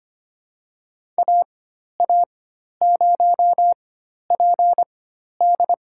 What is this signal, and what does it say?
Morse code, heard as a keyed tone, about 25 wpm, spaced as for 12 wpm, 700 Hz. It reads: AA0PD
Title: USA Call Signs - 5 Characters - 4x Character Spacing - 25wpm